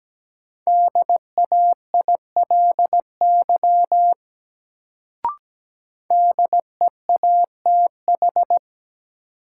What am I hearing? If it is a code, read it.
Morse code, heard as a keyed tone, about 17 wpm, 700 Hz. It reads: DAILY DEATH
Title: Sentences from Top 1100 Words - New Word Drill - 17wpm